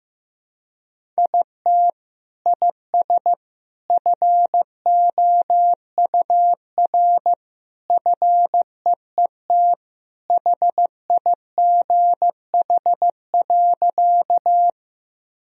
Morse code, heard as a keyed tone, about 14 wpm, 700 Hz. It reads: IT IS FOUR FEET HIGH.
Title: Sentences from Top 1000 Words - Review Part 1 - 15wpm